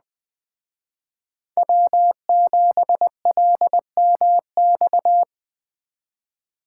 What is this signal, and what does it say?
Morse code, heard as a keyed tone, about 20 wpm, 700 Hz. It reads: W7LMX
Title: Speed Racing - USA 1x3 Call Signs 17wpm